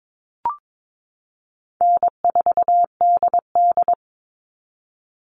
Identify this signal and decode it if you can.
Morse code, heard as a keyed tone, about 22 wpm, 700 Hz. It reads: N4DD
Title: Call Signs Repeated 3x - 22wpm